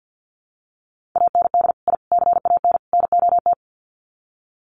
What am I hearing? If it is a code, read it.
Morse code, heard as a keyed tone, about 50 wpm, 700 Hz. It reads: AND I CAN NOT
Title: Sets of 4 Words 50wpm